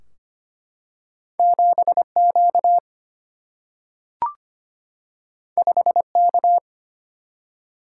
Morse code, heard as a keyed tone, about 25 wpm, 700 Hz. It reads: 7Q 5K
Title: Single-Letter Single-Number 25wpm